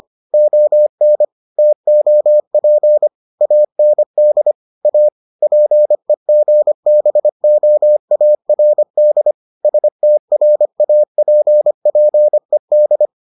Morse code, heard as a keyed tone, about 25 wpm, 600 Hz. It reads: ON TOP AND A PEGBOARD STRAPPED